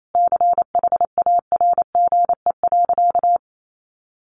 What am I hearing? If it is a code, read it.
Morse code, heard as a keyed tone, about 28 wpm, 700 Hz. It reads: CHARGE.
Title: Sentences from Top 900 Words 28wpm